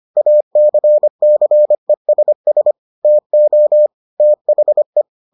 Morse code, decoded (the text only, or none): ACCESS TO THE